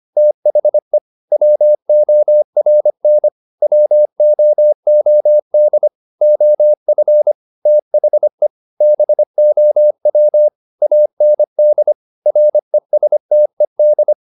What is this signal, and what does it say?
Morse code, heard as a keyed tone, about 25 wpm, 600 Hz. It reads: THE WORN WOOD OF THE BOW AND RESTED